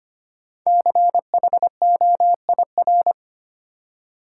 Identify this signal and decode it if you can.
Morse code, heard as a keyed tone, about 25 wpm, 700 Hz. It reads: CHOIR
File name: 5 Letter Words - First 3 Letters Given - 25wpm